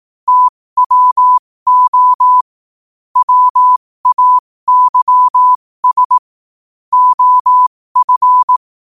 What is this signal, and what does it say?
Morse code, heard as a keyed tone, about 18 wpm, spaced as for 15 wpm, 1000 Hz. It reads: TWO WAYS OF
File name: qotdfiles / 15wpm